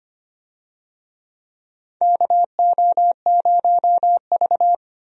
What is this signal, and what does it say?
Morse code, heard as a keyed tone, about 25 wpm, 700 Hz. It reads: KO0V
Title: USA Call Signs - 4 Characters - 25wpm